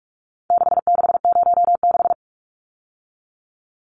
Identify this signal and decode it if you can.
Morse code, heard as a keyed tone, about 44 wpm, 700 Hz. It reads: 6606
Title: Numbers - 4-Digit 45wpm